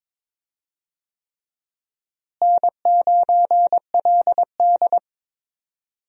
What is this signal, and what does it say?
Morse code, heard as a keyed tone, about 23 wpm, 700 Hz. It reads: N9LD
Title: USA Call Signs - 4 Characters - Repeated 3x - 22wpm